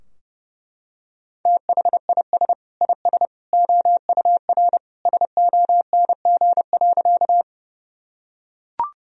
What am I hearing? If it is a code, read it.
Morse code, heard as a keyed tone, about 28 wpm, 700 Hz. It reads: THIS IS OUR SONG.
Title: Sentences from Top 1000 Words - Review Part 2 - 30wpm